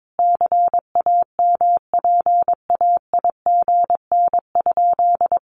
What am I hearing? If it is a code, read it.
Morse code, heard as a keyed tone, about 22 wpm, 700 Hz. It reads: CAMPAIGN?
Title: Sentences from Top 1200 Words 22wpm